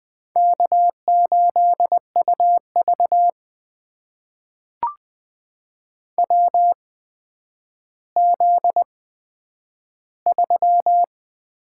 Morse code, heard as a keyed tone, about 20 wpm, spaced as for 10 wpm, 700 Hz. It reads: K8UV WZ3
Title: USA Call Signs - 4 Characters - 8x Character Spacing - 20wpm